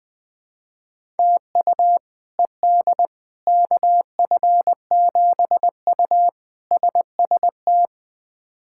Morse code, heard as a keyed tone, about 20 wpm, 700 Hz. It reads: TU ED KF7U SST
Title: K1USN SST QSO Practice - 20wpm